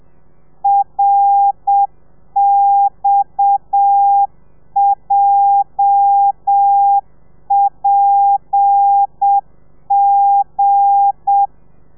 Morse code, heard as a keyed tone, about 7 wpm, 800 Hz. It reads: RXJPG
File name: telegrafia / i2tfj / Lez8 30